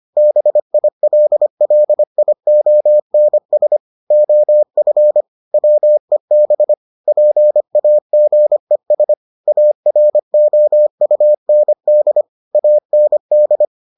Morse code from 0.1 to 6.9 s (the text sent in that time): BILLIONS OF WEB